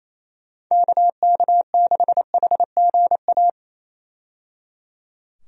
Morse code, heard as a keyed tone, about 28 wpm, 700 Hz. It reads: KK6HGA
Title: USA Call Signs - 6 Characters - 28wpm